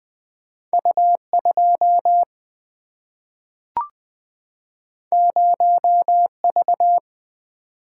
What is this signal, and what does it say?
Morse code, heard as a keyed tone, about 20 wpm, 700 Hz. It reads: U 2 0 V
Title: Single-Letter Single-Number 20wpm